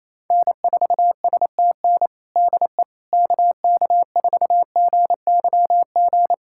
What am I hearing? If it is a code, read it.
Morse code, heard as a keyed tone, about 28 wpm, 700 Hz. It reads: N4STN DE KK4GYG
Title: USA Callsign DE Callsign 28wpm